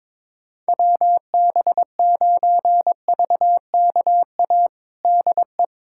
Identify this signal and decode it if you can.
Morse code, heard as a keyed tone, about 22 wpm, 700 Hz. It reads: WB9VKA DE
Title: USA Callsign DE Callsign 22wpm